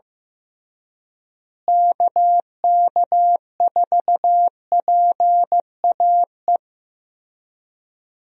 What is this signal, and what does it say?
Morse code, heard as a keyed tone, about 16 wpm, 700 Hz. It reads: KK4PAE
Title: USA Call Signs - 6 Characters - Repeated 3x - 15wpm